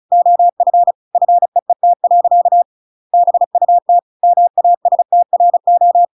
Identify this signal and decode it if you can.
Morse code, heard as a keyed tone, about 34 wpm, 700 Hz. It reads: OF FEET. BUT MASTRO
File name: Pinocchio / Pinocchio ch8 35 35 ik1hge